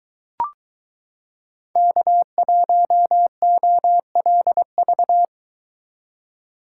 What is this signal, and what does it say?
Morse code, heard as a keyed tone, about 23 wpm, 700 Hz. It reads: K1OLV